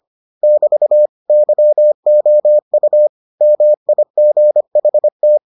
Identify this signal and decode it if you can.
Morse code, heard as a keyed tone, about 25 wpm, 600 Hz. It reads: <BT> YOU MIGHT